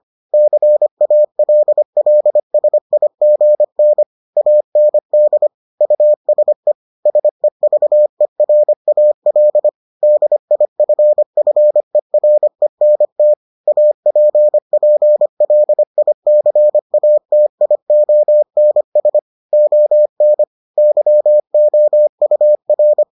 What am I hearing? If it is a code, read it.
Morse code, heard as a keyed tone, about 25 wpm, 600 Hz. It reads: CALLSIGN AND USE SEVERAL DIFFERENT APPLICATIONS ON YOUR